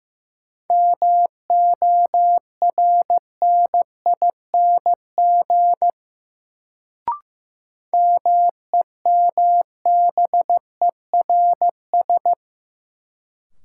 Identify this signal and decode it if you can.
Morse code, heard as a keyed tone, about 15 wpm, 700 Hz. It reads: MORNING MEMBERS
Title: Sets of 3 Words - New Word Drill - 15wpm